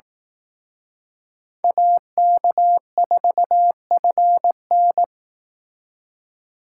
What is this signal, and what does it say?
Morse code, heard as a keyed tone, about 18 wpm, 700 Hz. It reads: AK4FN
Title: Speed Racing - USA 2x2 Call Signs 15wpm